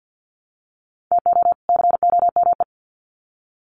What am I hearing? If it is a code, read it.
Morse code, heard as a keyed tone, about 50 wpm, 700 Hz. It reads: TO COME
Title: Sets of 2 Words 50wpm